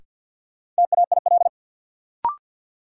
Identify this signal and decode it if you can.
Morse code, heard as a keyed tone, about 51 wpm, 700 Hz. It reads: TAIL
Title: Sentences from Top 1000 Words - Review Part 3 - New Word Drill - 50wpm